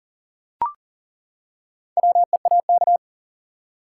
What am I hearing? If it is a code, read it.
Morse code, heard as a keyed tone, about 38 wpm, 700 Hz. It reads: WEAK
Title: Four Letter Words - 40wpm